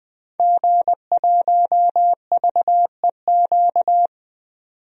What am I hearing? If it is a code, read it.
Morse code, heard as a keyed tone, about 20 wpm, 700 Hz. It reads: G1VEQ
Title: DXCC-100 Call Signs - 5 Characters - 20wpm